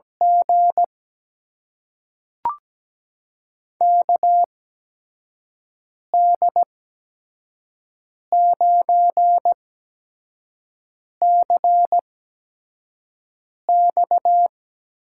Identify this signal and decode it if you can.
Morse code, heard as a keyed tone, about 17 wpm, spaced as for 5 wpm, 700 Hz. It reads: G KD9CX
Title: USA Call Signs - 8x Character Spacing - 17wpm